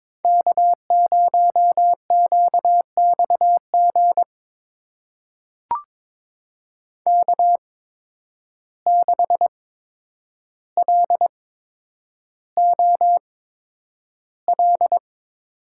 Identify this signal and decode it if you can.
Morse code, heard as a keyed tone, about 22 wpm, spaced as for 9 wpm, 700 Hz. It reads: K0QXG K6LOL
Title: USA Call Signs - 5 Characters - 8x Character Spacing - 22wpm